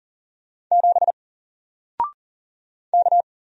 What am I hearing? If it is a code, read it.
Morse code, heard as a keyed tone, about 39 wpm, 700 Hz. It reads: 7 K
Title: Single Letter-Number Alt 40wpm